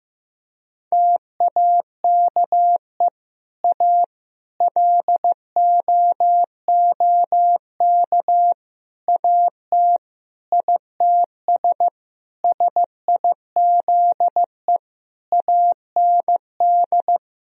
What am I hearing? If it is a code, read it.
Morse code, heard as a keyed tone, about 15 wpm, 700 Hz. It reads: TAKE A LOOK AT ITS SIZE AND
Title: Sentences from Top 1000 Words - Review Part 2 - 15wpm